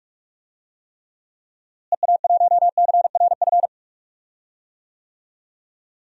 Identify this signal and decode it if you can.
Morse code, heard as a keyed tone, about 46 wpm, 700 Hz. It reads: EA1CRF
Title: DXCC-100 Call Signs - 6 Characters - Repeated 2x - 45wpm